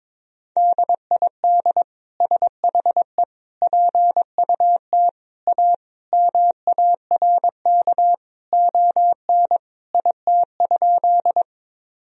Morse code, heard as a keyed tone, about 22 wpm, 700 Hz. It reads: DID SHE PUT A MARK ON IT?
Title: Sentences from Top 1000 Words - Review Part 1 - 22wpm